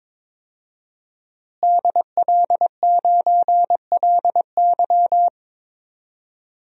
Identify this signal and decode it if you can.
Morse code, heard as a keyed tone, about 22 wpm, 700 Hz. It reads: DL9LY
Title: DXCC-100 Call Signs - 5 Characters - Repeated 2x - 22wpm